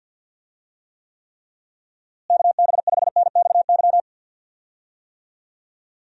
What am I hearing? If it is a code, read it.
Morse code, heard as a keyed tone, about 50 wpm, 700 Hz. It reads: KB5NXY
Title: USA Call Signs - 6 Characters - Repeated 3x - 50wpm